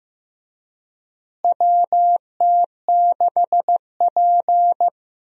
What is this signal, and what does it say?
Morse code, heard as a keyed tone, about 15 wpm, 700 Hz. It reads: WT6P
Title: USA Call Signs - 4 Characters - Repeated 2x - 15wpm